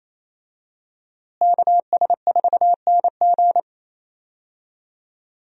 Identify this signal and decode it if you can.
Morse code, heard as a keyed tone, about 28 wpm, 700 Hz. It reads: KS4NG
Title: USA Call Signs - 5 Characters - Repeated 2x - 28wpm